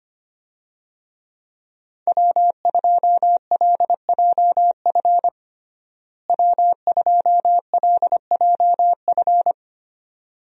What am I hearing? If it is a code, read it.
Morse code, heard as a keyed tone, about 25 wpm, 700 Hz. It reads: W2LJF W2LJF
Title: USA Call Signs - 5 Characters - Repeated 2x - 25wpm